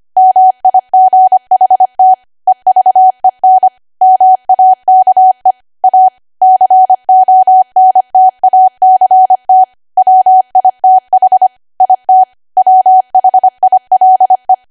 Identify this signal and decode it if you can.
Morse code, heard as a keyed tone, about 25 wpm, 750 Hz. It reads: MIGHT EVEN MAKE A CONTACT WITH IT WHILE
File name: cwa / W1AW-Level3 / 25WPM